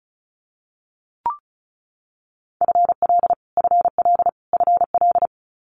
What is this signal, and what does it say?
Morse code, heard as a keyed tone, about 36 wpm, 700 Hz. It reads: FL FL FL
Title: Multi-letter Phonemes Repeated 3x 35wpm